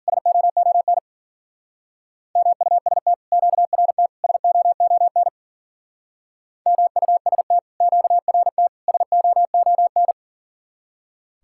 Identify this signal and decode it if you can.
Morse code, heard as a keyed tone, about 52 wpm, 700 Hz. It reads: SOON MUST QRT SOON MUST QRT SOON